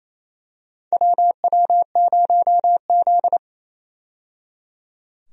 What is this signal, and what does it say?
Morse code, heard as a keyed tone, about 28 wpm, 700 Hz. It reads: WW0Z
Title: USA Call Signs - 4 Characters - 28wpm